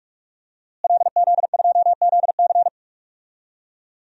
Morse code, heard as a keyed tone, about 46 wpm, 700 Hz. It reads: LZ2ZC
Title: DXCC-100 Call Signs - 5 Characters - Repeated 2x - 45wpm